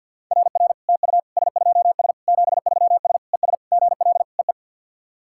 Morse code, heard as a keyed tone, about 53 wpm, 700 Hz. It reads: RR TU S2S 73S ES GL EE
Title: SOTA QSO Practice 50wpm